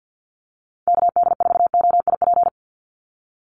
Morse code, heard as a keyed tone, about 50 wpm, 700 Hz. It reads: KD4OIP
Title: USA Call Signs 50wpm